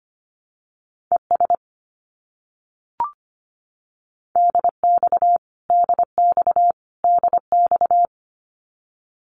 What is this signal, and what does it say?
Morse code, heard as a keyed tone, about 25 wpm, 700 Hz. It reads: ES DX DX DX
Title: Abbreviations Repeated 3x - 25wpm